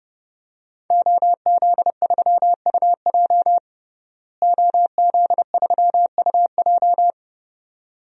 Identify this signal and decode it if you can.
Morse code, heard as a keyed tone, about 30 wpm, 700 Hz. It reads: OZ3UJ OZ3UJ
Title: DXCC-100 Call Signs - 5 Characters - Repeated 2x - 30wpm